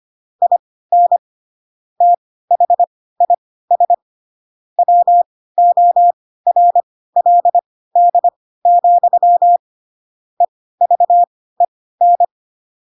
Morse code, heard as a keyed tone, about 25 wpm, spaced as for 16 wpm, 700 Hz. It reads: IN THIS WORLD, EVEN
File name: Pinocchio / Pinocchio ch7 10 25 ik1hge